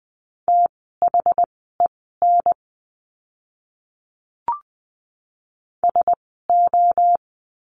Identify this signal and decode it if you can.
Morse code, heard as a keyed tone, about 20 wpm, spaced as for 14 wpm, 700 Hz. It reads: THEN SO
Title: 230 Lesson 37f - Letter x - Words 20wpm